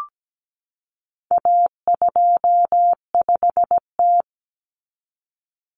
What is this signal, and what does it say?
Morse code, heard as a keyed tone, about 17 wpm, 700 Hz. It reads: A25T
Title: Common Cut Numbers - 4-Digit 17wpm